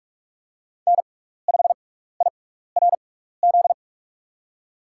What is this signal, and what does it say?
Morse code, heard as a keyed tone, about 47 wpm, spaced as for 16 wpm, 700 Hz. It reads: N5IRZ
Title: USA Call Signs - 5 Characters - 6x Character Spacing - 45wpm